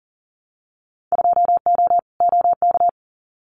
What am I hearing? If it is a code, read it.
Morse code, heard as a keyed tone, about 40 wpm, 700 Hz. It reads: 2O OK
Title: Winter Field Day Exchanges Encore 40wpm